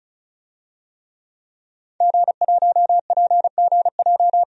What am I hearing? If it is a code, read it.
Morse code, heard as a keyed tone, about 36 wpm, 700 Hz. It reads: G1PGJ
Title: DXCC-100 Call Signs - 5 Characters - 35wpm